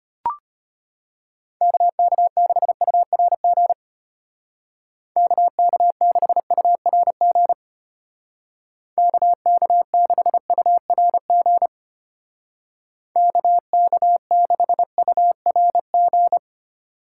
Morse code, heard as a keyed tone, about 30 wpm, 700 Hz. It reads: KK6URG KK6URG KK6URG KK6URG
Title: Speed Racing - USA 2x3 Call Signs 25wpm